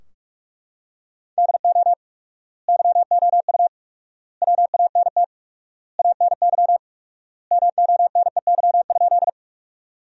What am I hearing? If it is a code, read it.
Morse code, heard as a keyed tone, about 44 wpm, 700 Hz. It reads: DO YOU WANT ANY MONEY?
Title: 4x Word-Spacing - Sentences from Top 400 Words - 45wpm